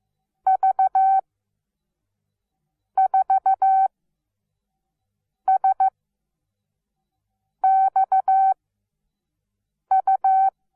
Morse code, heard as a keyed tone, about 15 wpm, spaced as for 5 wpm, 770 Hz. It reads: V4SXU
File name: Onlinelehrgang / morsen / DJ4UF Morsen17